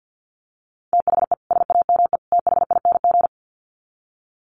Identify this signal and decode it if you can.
Morse code, heard as a keyed tone, about 50 wpm, 700 Hz. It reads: THE SAME THING
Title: Sets of 3 Words 50wpm